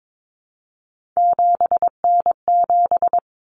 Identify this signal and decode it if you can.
Morse code, heard as a keyed tone, about 22 wpm, 700 Hz. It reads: 7N7
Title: Common Cut Numbers - 3-Digit 22wpm